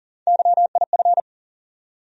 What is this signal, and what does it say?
Morse code, heard as a keyed tone, about 40 wpm, 700 Hz. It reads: YIF
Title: USA Call Sign Postfixes - 3 Letters 40wpm